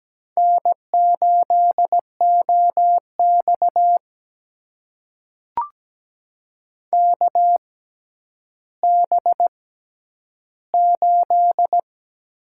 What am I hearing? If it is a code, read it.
Morse code, heard as a keyed tone, about 17 wpm, spaced as for 10 wpm, 700 Hz. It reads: N8OX KB8
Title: USA Call Signs - 6x Character Spacing - 17wpm